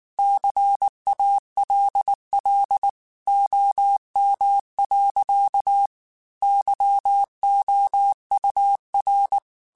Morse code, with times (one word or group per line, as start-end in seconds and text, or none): 0.2-2.9 CALL
3.3-5.9 OM.
6.4-9.4 YOUR